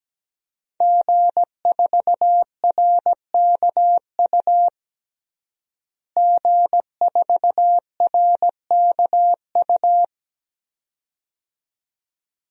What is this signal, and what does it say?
Morse code, heard as a keyed tone, about 17 wpm, 700 Hz. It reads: G4RKU G4RKU
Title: DXCC-100 Call Signs - 5 Characters - Repeated 2x - 17wpm